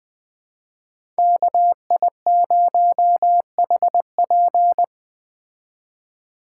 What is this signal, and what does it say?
Morse code, heard as a keyed tone, about 21 wpm, 700 Hz. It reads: KI0HP
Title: Speed Racing - USA 2x2 Call Signs 17wpm